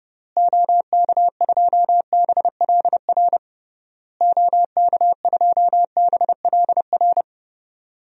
Morse code, heard as a keyed tone, about 30 wpm, 700 Hz. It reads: OK2BLR OK2BLR
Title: DXCC-100 Call Signs - 6 Characters - Repeated 3x - 30wpm